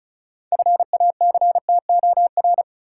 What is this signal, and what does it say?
Morse code, heard as a keyed tone, about 34 wpm, 700 Hz. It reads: FACTOR
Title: Six Letter Words - 35wpm